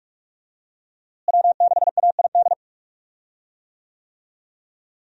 Morse code, heard as a keyed tone, about 46 wpm, 700 Hz. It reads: W6AID
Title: USA Call Signs - 5 Characters - Repeated 3x - 45wpm